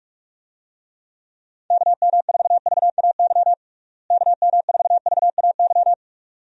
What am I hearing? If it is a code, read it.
Morse code, heard as a keyed tone, about 41 wpm, 700 Hz. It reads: KM4VAY KM4VAY
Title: USA Call Signs - 6 Characters - Repeated 2x - 45wpm